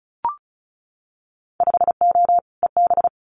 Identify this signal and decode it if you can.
Morse code, heard as a keyed tone, about 36 wpm, 700 Hz. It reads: 5O EB
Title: Winter Field Day Exchanges Encore 35wpm